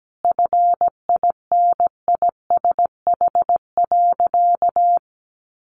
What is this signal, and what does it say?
Morse code, heard as a keyed tone, about 17 wpm, 700 Hz. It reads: FINISH.
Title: Sentences from Top 1200 Words 17wpm